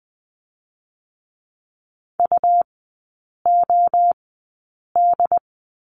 Morse code, heard as a keyed tone, about 20 wpm, spaced as for 8 wpm, 700 Hz. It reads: UOD
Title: 127 Lesson 21e - Number 6 - Three Characters 20wpm